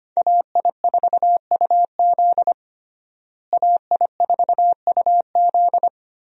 Morse code, heard as a keyed tone, about 25 wpm, 700 Hz. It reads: AI4UZ AI4UZ
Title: USA Call Signs - 5 Characters - Repeated 3x - 25wpm